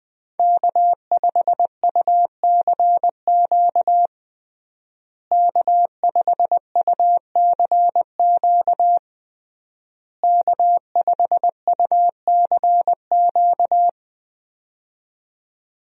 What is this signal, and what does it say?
Morse code, heard as a keyed tone, about 20 wpm, 700 Hz. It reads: K5UCQ K5UCQ K5UCQ
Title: USA Call Signs - 5 Characters - Repeated 3x - 20wpm